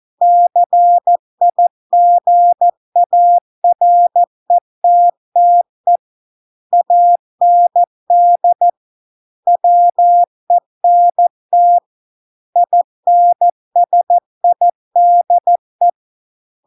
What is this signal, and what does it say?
Morse code, heard as a keyed tone, about 14 wpm, 700 Hz. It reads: CIGARETTE AND WENT INSIDE